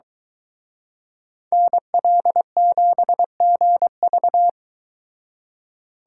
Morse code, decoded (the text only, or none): NL7GV